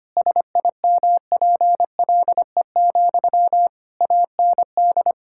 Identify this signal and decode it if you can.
Morse code, heard as a keyed tone, about 25 wpm, 700 Hz. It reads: SIMPLE, AND
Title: Sentences from Top 1000 Words - Review Part 6 - 25wpm